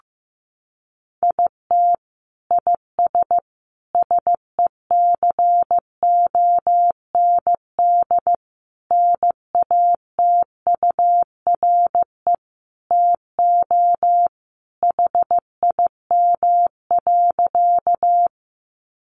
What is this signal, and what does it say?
Morse code, heard as a keyed tone, about 15 wpm, 700 Hz. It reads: IT IS SECOND NATURE TO HIM.
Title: Sentences from Top 600 Words 15wpm